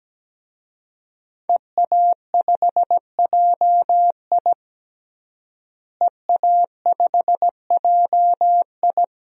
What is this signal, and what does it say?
Morse code, heard as a keyed tone, about 17 wpm, 700 Hz. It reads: EA5JI EA5JI
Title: DXCC-100 Call Signs - 5 Characters - Repeated 3x - 17wpm